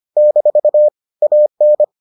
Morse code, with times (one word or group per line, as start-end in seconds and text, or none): 0.2-0.9 -
1.2-1.8 AN